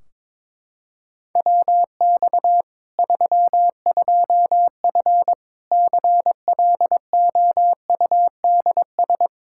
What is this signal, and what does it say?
Morse code, heard as a keyed tone, about 22 wpm, 700 Hz. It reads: WX 32F CLOUDS
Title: Practice QSO - Exchange 2 - Info - 22wpm